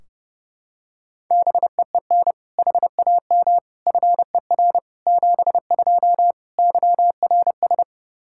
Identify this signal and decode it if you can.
Morse code, heard as a keyed tone, about 28 wpm, 700 Hz. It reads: BEEN HAM FER 72 YRS
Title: Practice QSO - Exchange 3 - More Info - 30wpm